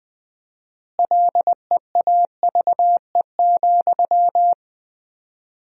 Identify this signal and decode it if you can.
Morse code, heard as a keyed tone, about 20 wpm, 700 Hz. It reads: LEAVE,
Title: 4x Word-Spacing - Sentences from Top 400 Words - 20wpm